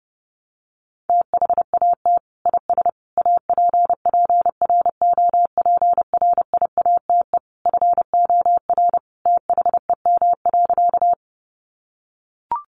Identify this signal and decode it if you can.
Morse code, heard as a keyed tone, about 30 wpm, 700 Hz. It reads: THAT IS APPROPRIATE FOR THEM.
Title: Sentences from Top 900 Words 30wpm